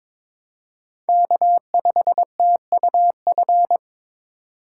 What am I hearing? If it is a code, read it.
Morse code, heard as a keyed tone, about 26 wpm, 700 Hz. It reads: K5TUF
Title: Speed Racing - USA 1x3 Call Signs 22wpm